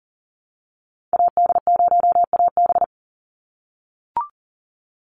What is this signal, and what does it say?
Morse code, heard as a keyed tone, about 38 wpm, 700 Hz. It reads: AD0AB
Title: Call Signs - 40wpm